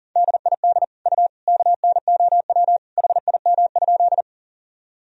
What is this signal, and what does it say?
Morse code, heard as a keyed tone, about 40 wpm, 700 Hz. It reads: DID U KNOW HIM?
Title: Sentences from Top 100 Words and Abbreviations 40wpm